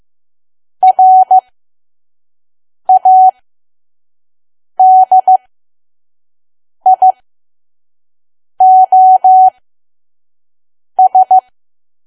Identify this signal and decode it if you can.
Morse code, heard as a keyed tone, about 15 wpm, spaced as for 5 wpm, 750 Hz. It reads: RADIOS
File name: Archive / 5 WPM files / PM